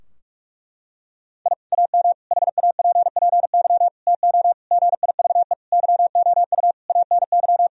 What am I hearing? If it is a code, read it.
Morse code, heard as a keyed tone, about 46 wpm, 700 Hz. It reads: I AM HAPPY TO GIVE YOU ANY
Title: Sentences from Top 1000 Words - Review Part 3 - 45wpm